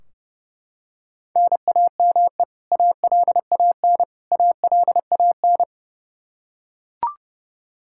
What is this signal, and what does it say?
Morse code, heard as a keyed tone, about 30 wpm, 700 Hz. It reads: NAME ALAN ALAN
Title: Practice QSO - Exchange 1 - Beginning - 30wpm